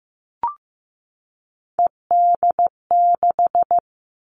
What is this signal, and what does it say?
Morse code, heard as a keyed tone, about 15 wpm, 700 Hz. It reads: ED6
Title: Callsign Prefixes for Top 100 DXCC - 15wpm